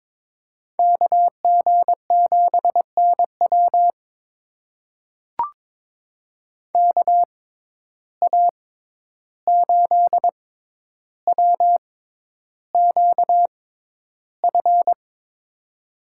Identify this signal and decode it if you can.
Morse code, heard as a keyed tone, about 22 wpm, spaced as for 11 wpm, 700 Hz. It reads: KG7NW KA8WQF
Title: USA Call Signs - 6x Character Spacing - 22wpm